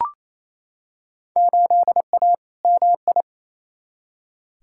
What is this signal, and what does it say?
Morse code, heard as a keyed tone, about 28 wpm, 700 Hz. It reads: 8A MI
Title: Field Day Exchanges Encore - 28wpm